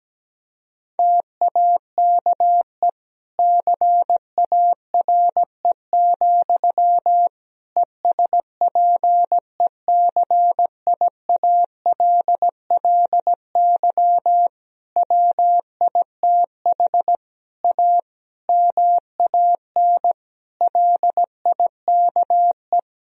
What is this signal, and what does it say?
Morse code, heard as a keyed tone, about 17 wpm, 700 Hz. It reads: TAKE CARE, ESPECIALLY WITH A MAN LIKE
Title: Sentences from Top 1000 Words - Review Part 4 - 17wpm